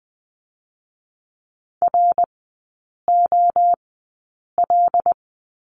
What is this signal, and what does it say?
Morse code, heard as a keyed tone, about 20 wpm, spaced as for 8 wpm, 700 Hz. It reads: ROL
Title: 107 Lesson 18e - Letter m - Three Characters 20wpm